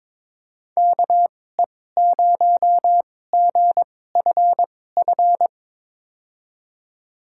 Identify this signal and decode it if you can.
Morse code, heard as a keyed tone, about 22 wpm, spaced as for 16 wpm, 700 Hz. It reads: KE0GFF
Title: USA Call Signs - 2x Character Spacing - 22wpm